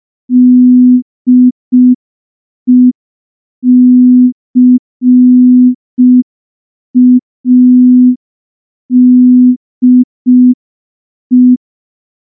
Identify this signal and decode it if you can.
Morse code, heard as a keyed tone, about 5 wpm, 250 Hz. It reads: DECADE
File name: morse / morse-training-05WPM